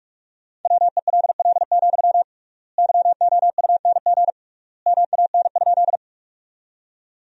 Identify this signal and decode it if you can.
Morse code, heard as a keyed tone, about 44 wpm, 700 Hz. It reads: WELL, YOUNG MAN?
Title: 3x Word-Spacing - Sentences from Top 400 Words - 45wpm